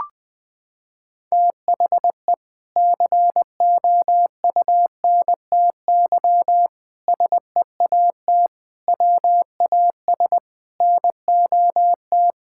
Morse code, heard as a keyed tone, about 21 wpm, 700 Hz. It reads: THE COUNTY SEAT WAS NOT
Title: Sentences from Top 1600 Words 20wpm